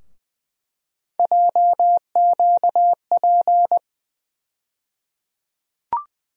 Morse code, heard as a keyed tone, about 21 wpm, 700 Hz. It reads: JQP
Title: USA Call Sign Postfixes - 3 Letters 20wpm